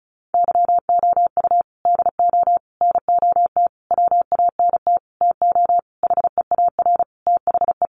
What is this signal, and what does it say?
Morse code, heard as a keyed tone, about 34 wpm, 700 Hz. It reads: YOU DO NOT WANT TO HEAR THE
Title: Sentences from Top 1400 Words 35wpm